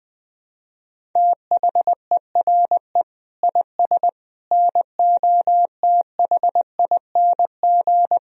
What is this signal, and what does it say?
Morse code, heard as a keyed tone, about 20 wpm, 700 Hz. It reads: THERE IS NOTHING
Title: Sentences from Top 1000 Words - Review Part 1 - 20wpm